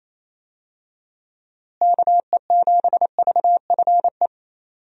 Morse code, heard as a keyed tone, about 28 wpm, 700 Hz. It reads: KE7VFE
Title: USA Call Signs - 6 Characters - Repeated 3x - 28wpm